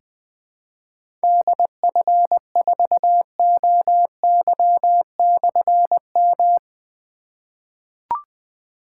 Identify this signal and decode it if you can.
Morse code, heard as a keyed tone, about 20 wpm, 700 Hz. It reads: DF4OY/M
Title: Difficult Call Signs 20wpm